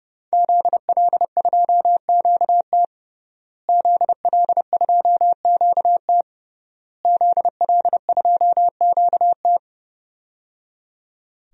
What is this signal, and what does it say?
Morse code, heard as a keyed tone, about 30 wpm, 700 Hz. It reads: ZL2QT ZL2QT ZL2QT
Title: DXCC-100 Call Signs - 5 Characters - Repeated 3x - 30wpm